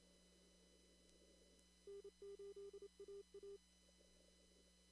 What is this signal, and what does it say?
Morse code, heard as a keyed tone, about 27 wpm, 400 Hz. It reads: N8AA
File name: CW / kt5j wpx cw-op-k5pi